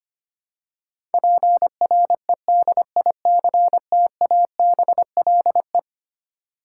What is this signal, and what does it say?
Morse code, heard as a keyed tone, about 25 wpm, 700 Hz. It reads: PREDICTABLE
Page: morse-code-ninja-practice-25wpm